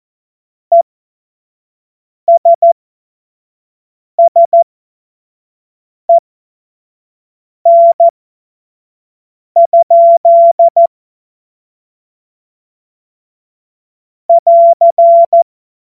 Morse code, heard as a keyed tone, about 14 wpm, spaced as for 5 wpm, 680 Hz. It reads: ESSEN? <AR>
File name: Morsekurs / Witze / Witz 11. Januar 2022, 25 F70